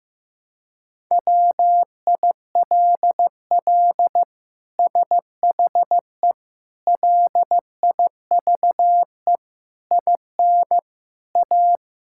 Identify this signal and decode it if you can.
Morse code, heard as a keyed tone, about 15 wpm, 700 Hz. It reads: WILL SHE LIVE IN A
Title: Sentences from Top 1000 Words - Review Part 6 - 15wpm